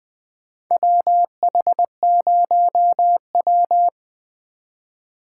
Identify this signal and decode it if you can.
Morse code, heard as a keyed tone, about 20 wpm, 700 Hz. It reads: WH0W
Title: USA Call Signs - 8x Character Spacing - 20wpm